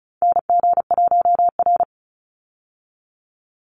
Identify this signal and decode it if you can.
Morse code, heard as a keyed tone, about 34 wpm, 700 Hz. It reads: NG1R
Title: Call Signs Repeated 2x - 35wpm